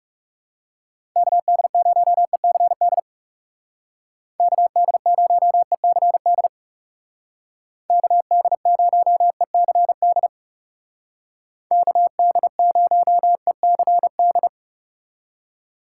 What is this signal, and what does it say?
Morse code, heard as a keyed tone, about 37 wpm, 700 Hz. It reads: KD0ECD KD0ECD KD0ECD KD0ECD
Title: Speed Racing - USA 2x3 Call Signs 30wpm